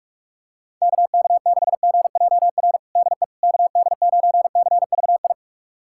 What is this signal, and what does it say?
Morse code, heard as a keyed tone, about 41 wpm, 700 Hz. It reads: KK6GJR DE KD9CVI
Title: USA Callsign DE Callsign - 6 Characters - 45wpm